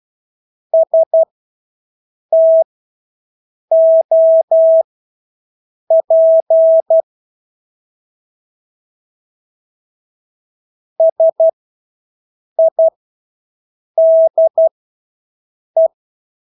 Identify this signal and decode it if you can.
Morse code, heard as a keyed tone, about 12 wpm, spaced as for 6 wpm, 650 Hz. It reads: STOP SIDE
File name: Engelske ord 12-6 wpm